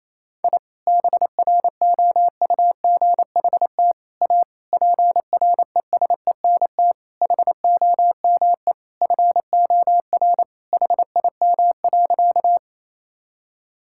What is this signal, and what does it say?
Morse code, heard as a keyed tone, about 28 wpm, 700 Hz. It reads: I BROUGHT A PRESENT HOME FOR HIM.
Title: Sentences from Top 1000 Words - Review Part 5 - 28wpm